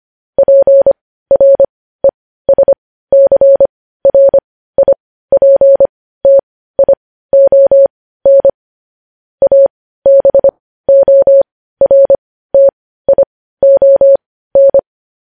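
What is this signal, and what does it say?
Morse code, heard as a keyed tone, about 25 wpm, spaced as for 15 wpm, 550 Hz. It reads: PRESCRIPTION ABORTION
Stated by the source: Morse code news-headline practice sent at 15 words per minute.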